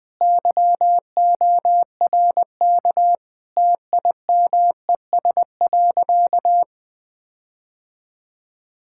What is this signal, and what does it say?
Morse code, heard as a keyed tone, about 20 wpm, 700 Hz. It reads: YORK TIMES.
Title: Sentences with 5-Word Phrases - 20wpm